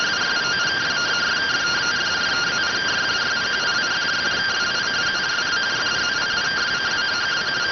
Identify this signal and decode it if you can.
Two-tone teleprinter signal, about 170 Hz shift.